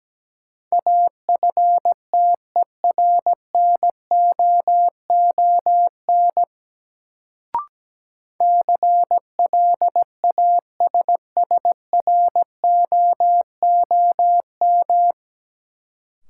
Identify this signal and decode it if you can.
Morse code, heard as a keyed tone, about 17 wpm, 700 Hz. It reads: AFTERNOON CLASSROOM
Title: Sentences from Top 1500 Words - New Word Drill Alt - 17wpm